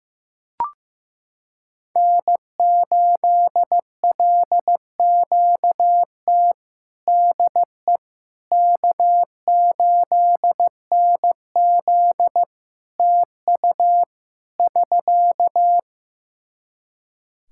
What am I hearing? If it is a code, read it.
Morse code, heard as a keyed tone, about 16 wpm, 700 Hz. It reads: N8LQT DE K8NZ TU <SK>
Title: Practice QSO - Exchange 4 - Ending - 15wpm